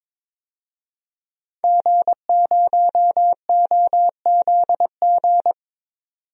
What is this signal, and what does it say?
Morse code, heard as a keyed tone, about 22 wpm, 700 Hz. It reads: G0OZG
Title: DXCC-100 Call Signs - 5 Characters - 22wpm